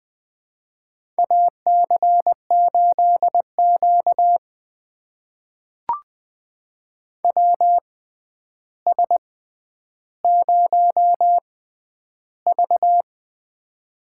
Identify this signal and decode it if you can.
Morse code, heard as a keyed tone, about 20 wpm, spaced as for 10 wpm, 700 Hz. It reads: AC8Q WS0V
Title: USA Call Signs - 4 Characters - 6x Character Spacing - 20wpm